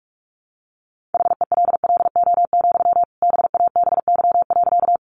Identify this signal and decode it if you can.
Morse code, heard as a keyed tone, about 44 wpm, 700 Hz. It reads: HELLO, BABY.
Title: Sentences from Top 1300 Words 45wpm